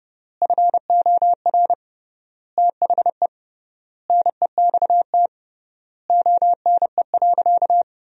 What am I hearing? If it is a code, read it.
Morse code, heard as a keyed tone, about 30 wpm, 700 Hz. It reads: FOR THE NEXT ONE.
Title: 3x Word-Spacing - Sentences from Top 400 Words - 30wpm